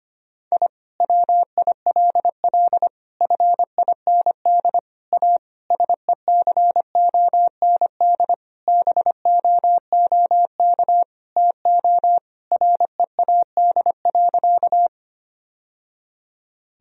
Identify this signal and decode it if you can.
Morse code, heard as a keyed tone, about 25 wpm, 700 Hz. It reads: I WILL FIND A SECOND BOOK TO READ.
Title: Sentences from Top 1000 Words - Review Part 1 - 25wpm